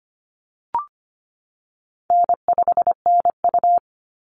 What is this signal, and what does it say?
Morse code, heard as a keyed tone, about 25 wpm, 700 Hz. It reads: N5NU
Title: Call Signs - 25wpm